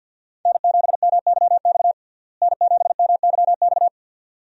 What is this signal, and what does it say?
Morse code, heard as a keyed tone, about 52 wpm, 700 Hz. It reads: N7MYX N7MYX
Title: USA Call Signs - 5 Characters - Repeated 2x - 50wpm